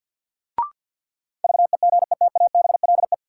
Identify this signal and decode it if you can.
Morse code, heard as a keyed tone, about 53 wpm, 700 Hz. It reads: VEGETABLE
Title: Sentences from Top 1300 Words - New Word Drill - 50wpm